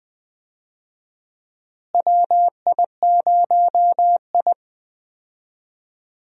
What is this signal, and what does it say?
Morse code, heard as a keyed tone, about 20 wpm, 700 Hz. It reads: WI0I
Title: USA Call Signs - 4 Characters - Repeated 2x - 20wpm